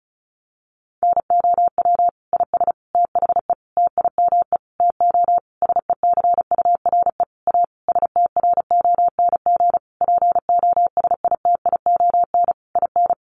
Morse code, heard as a keyed tone, about 34 wpm, 700 Hz. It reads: NOW IS THE TIME TO SECURE A STRONG POSITION IN
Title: Sentences from Top 1200 Words 35wpm